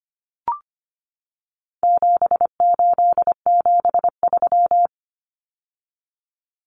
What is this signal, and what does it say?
Morse code, heard as a keyed tone, about 25 wpm, 700 Hz. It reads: 7873
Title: Common Cut Numbers - 4-Digit 25wpm